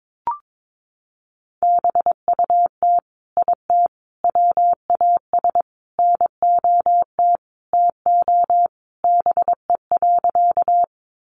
Morse code, heard as a keyed tone, about 22 wpm, 700 Hz. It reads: BUT IT WAS NOT TO BE.
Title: Sentences from Top 600 Words 22wpm